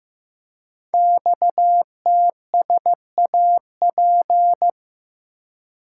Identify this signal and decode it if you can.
Morse code, heard as a keyed tone, about 15 wpm, 700 Hz. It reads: XTSAP
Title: Code Groups - 5 Letters - 15wpm